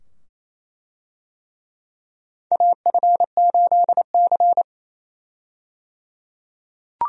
Morse code, heard as a keyed tone, about 28 wpm, 700 Hz. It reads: AF8C
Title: USA Call Signs - 4 Characters - 28wpm